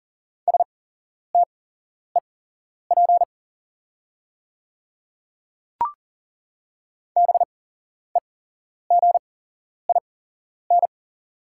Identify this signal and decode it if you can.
Morse code, heard as a keyed tone, about 40 wpm, spaced as for 11 wpm, 690 Hz. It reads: STEP BEGIN
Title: 8x Character Spacing - Top 500 Words - 40wpm